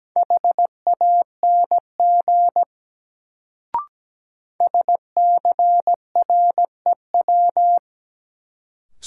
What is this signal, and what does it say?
Morse code, heard as a keyed tone, about 17 wpm, 700 Hz. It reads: HANG SCREW
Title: Sentences from Top 1600 Words - New Word Drill Alt - 17wpm